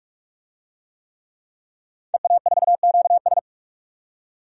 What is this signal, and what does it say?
Morse code, heard as a keyed tone, about 43 wpm, 700 Hz. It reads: EA4QS
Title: DXCC-100 Call Signs - 5 Characters - 45wpm